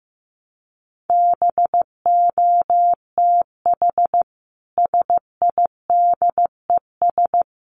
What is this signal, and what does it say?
Morse code, heard as a keyed tone, about 15 wpm, 700 Hz. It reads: BOTH SIDES
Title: Sets of 4 Words 15wpm